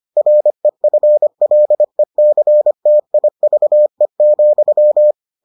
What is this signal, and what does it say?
Morse code, heard as a keyed tone, about 25 wpm, 600 Hz. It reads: REFLECTIVE,